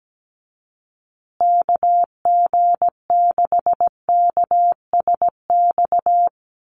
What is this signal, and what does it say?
Morse code, heard as a keyed tone, about 17 wpm, 700 Hz. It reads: KG6KSX
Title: USA Call Signs Repeated 3x 17wpm